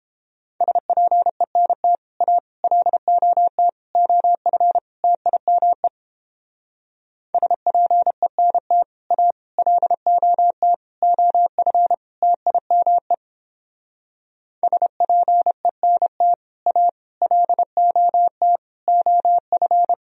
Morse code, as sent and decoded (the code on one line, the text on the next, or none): ... .--. -. - / .- / .-.. --- - / --- ..-. / - .. -- . / ... .--. -. - / .- / .-.. --- - / --- ..-. / - .. -- . / ... .--. -. - / .- / .-.. --- - / --- ..-.
SPENT A LOT OF TIME SPENT A LOT OF TIME SPENT A LOT OF